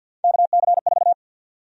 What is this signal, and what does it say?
Morse code, heard as a keyed tone, about 52 wpm, 700 Hz. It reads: KC4
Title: USA Call Sign Prefixes - 50wpm